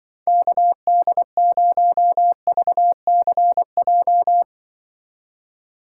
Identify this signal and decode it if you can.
Morse code, heard as a keyed tone, about 25 wpm, 700 Hz. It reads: KD0VCJ
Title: Speed Racing - USA 2x3 Call Signs 20wpm